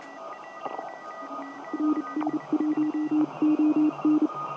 Morse code, heard as a keyed tone, about 31 wpm, spaced as for 27 wpm, 300 Hz, with radio noise. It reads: NS1ON
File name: tt8tt-10m-cw-here-in-jn90ab